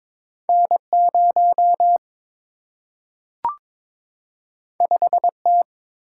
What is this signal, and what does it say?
Morse code, heard as a keyed tone, about 22 wpm, 700 Hz. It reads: N 0 5 T
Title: Single-Letter Single-Number 22wpm